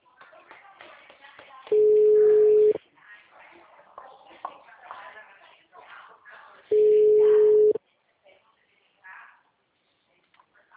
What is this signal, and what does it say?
Ringback tone (Europe, 425 Hz), with speech.